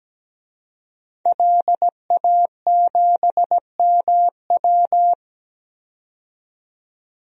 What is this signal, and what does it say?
Morse code, heard as a keyed tone, about 17 wpm, 700 Hz. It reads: LA7MW
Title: DXCC-100 Call Signs - 5 Characters - 17wpm